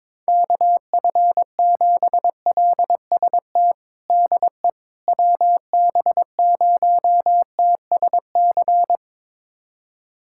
Morse code, heard as a keyed tone, about 21 wpm, 700 Hz. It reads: KF7LST DE WB0TSC
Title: USA Callsign DE Callsign 22wpm